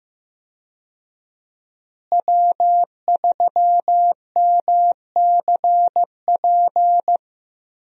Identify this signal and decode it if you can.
Morse code, heard as a keyed tone, about 15 wpm, 700 Hz. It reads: W3MCP
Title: USA Call Signs - 5 Characters - Repeated 3x - 15wpm